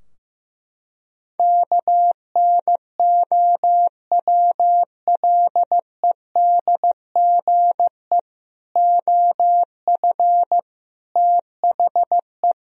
Morse code, heard as a keyed tone, about 15 wpm, 700 Hz. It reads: KNOWLEDGE OF THE
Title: Sentences from Top 1000 Words - Review Part 3 - 15wpm